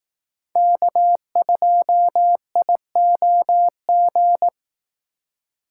Morse code, heard as a keyed tone, about 18 wpm, 700 Hz. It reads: K2IOG
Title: Speed Racing - USA 1x3 Call Signs 15wpm